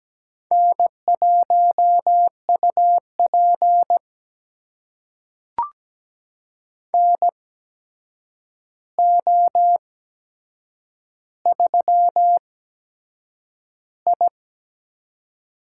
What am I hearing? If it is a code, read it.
Morse code, heard as a keyed tone, about 17 wpm, spaced as for 7 wpm, 700 Hz. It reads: N1UP NO3I
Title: USA Call Signs - 4 Characters - 8x Character Spacing - 17wpm